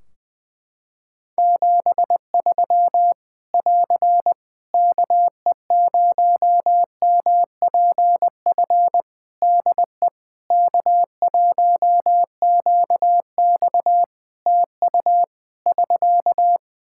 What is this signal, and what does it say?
Morse code, heard as a keyed tone, about 20 wpm, 700 Hz. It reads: 73 <AR> KE0MPF DE K1QX TU <SK>
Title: Practice QSO - Exchange 4 - Ending - 20wpm